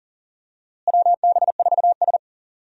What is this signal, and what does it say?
Morse code, heard as a keyed tone, about 40 wpm, 700 Hz. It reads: WB4S
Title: USA Call Signs - 4 Characters - Repeated 2x - 40wpm